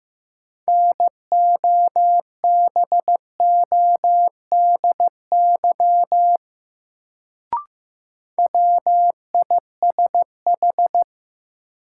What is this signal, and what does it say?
Morse code, heard as a keyed tone, about 15 wpm, 700 Hz. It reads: NOBODY WISH
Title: Sentences from Top 1600 Words - New Word Drill - 15wpm